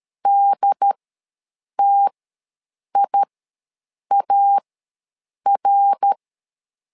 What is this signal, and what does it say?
Morse code, heard as a keyed tone, about 13 wpm, spaced as for 7 wpm, 800 Hz. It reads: DTIAR
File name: LEZIONI / 3 MWRO / 3 MWR ws 12wpm 1809